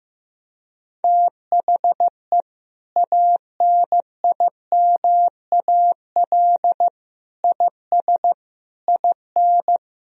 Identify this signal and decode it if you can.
Morse code, heard as a keyed tone, about 15 wpm, 700 Hz. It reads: THE ANIMAL IS IN
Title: Sentences from Top 1000 Words - Review Part 3 - 15wpm